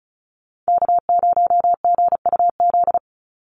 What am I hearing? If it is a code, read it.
Morse code, heard as a keyed tone, about 34 wpm, 700 Hz. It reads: K0GUZ
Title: Call Signs Repeated 2x - 35wpm